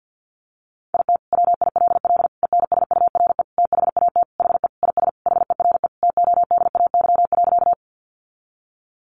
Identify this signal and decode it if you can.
Morse code, heard as a keyed tone, about 50 wpm, 700 Hz. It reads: IT WILL ENSURE THAT HE IS HERE TODAY.
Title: Sentences from Top 1100 Words 50wpm